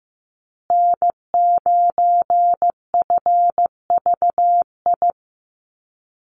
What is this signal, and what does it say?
Morse code, heard as a keyed tone, about 15 wpm, 700 Hz. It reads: N9FVI
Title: USA Call Signs Repeated 3x 15wpm